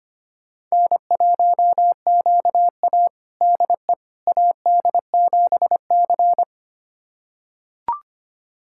Morse code, heard as a keyed tone, about 25 wpm, 700 Hz. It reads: N1QA DE AD7C
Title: USA Callsign DE Callsign - 4 Characters - 25wpm